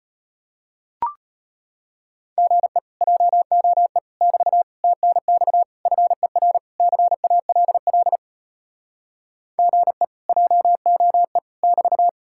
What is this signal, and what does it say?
Morse code, heard as a keyed tone, about 33 wpm, 700 Hz. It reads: GE JOE <BT> TNX FER CALL GE JOE <BT>